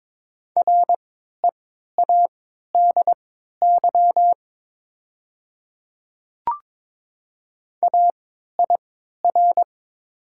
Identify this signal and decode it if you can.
Morse code, heard as a keyed tone, about 22 wpm, spaced as for 12 wpm, 700 Hz. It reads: READY AIR
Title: 3x Character Spacing - Top 500 Words - 22wpm